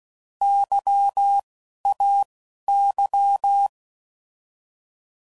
Morse code, heard as a keyed tone, about 16 wpm, spaced as for 11 wpm, 800 Hz. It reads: YAY
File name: Prefixes / By First Letter / Ys